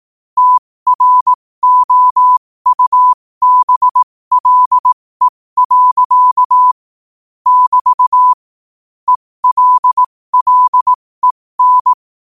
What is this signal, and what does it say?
Morse code, heard as a keyed tone, about 18 wpm, spaced as for 15 wpm, 1000 Hz. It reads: TROUBLE. <BT> ELLEN
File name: qotdfiles / 15wpm